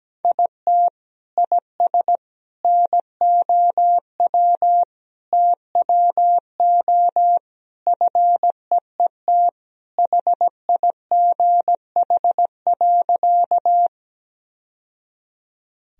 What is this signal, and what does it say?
Morse code, heard as a keyed tone, about 17 wpm, 700 Hz. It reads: IT IS NOW TWO FEET HIGH.
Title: Sentences from Top 1000 Words - Review Part 1 - 17wpm